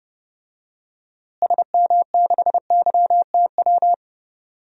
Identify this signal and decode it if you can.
Morse code, heard as a keyed tone, about 31 wpm, 700 Hz. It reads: SM6YTW
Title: DXCC-100 Call Signs - 6 Characters - Repeated 2x - 30wpm